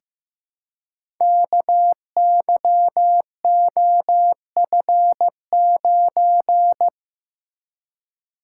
Morse code, heard as a keyed tone, about 15 wpm, 700 Hz. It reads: KYOF9
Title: Code Groups - 5 Letter:Numbers - 15wpm